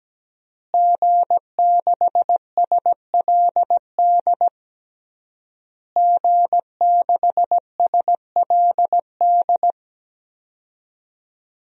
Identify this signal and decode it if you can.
Morse code, heard as a keyed tone, about 17 wpm, 700 Hz. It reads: G6SLD G6SLD
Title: DXCC-100 Call Signs - 5 Characters - Repeated 3x - 17wpm